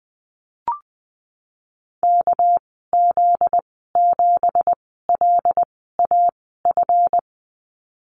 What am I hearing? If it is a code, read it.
Morse code, heard as a keyed tone, about 20 wpm, spaced as for 15 wpm, 700 Hz. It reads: KZ7LAF
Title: 245 Lesson 39g - Letter z - Call Signs 20wpm